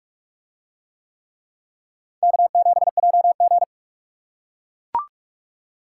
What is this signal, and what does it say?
Morse code, heard as a keyed tone, about 44 wpm, 700 Hz. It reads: K7JG